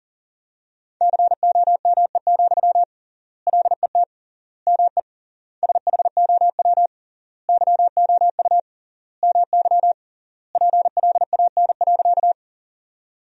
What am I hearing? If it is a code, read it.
Morse code, heard as a keyed tone, about 40 wpm, 700 Hz. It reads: COME, LET ME SHOW YOU MY PLAN.
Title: 3x Word-Spacing - Sentences from Top 500 Words - 40wpm